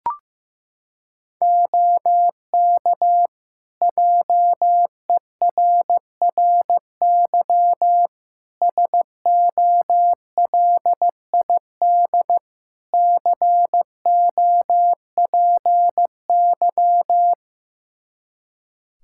Morse code, heard as a keyed tone, about 15 wpm, 700 Hz. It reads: OK JERRY SOLID COPY
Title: Practice QSO - Exchange 3 - More Info - 15wpm